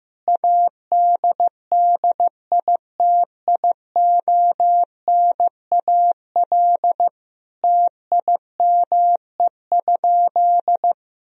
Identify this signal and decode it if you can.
Morse code, heard as a keyed tone, about 15 wpm, 700 Hz. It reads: ADDITIONAL TIME?
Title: Sentences from Top 1000 Words - Review Part 4 - 15wpm